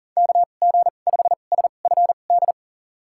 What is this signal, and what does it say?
Morse code, heard as a keyed tone, about 37 wpm, spaced as for 28 wpm, 700 Hz. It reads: KG5SFD
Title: USA Call Signs - 6 Characters - 2x Character Spacing - 40wpm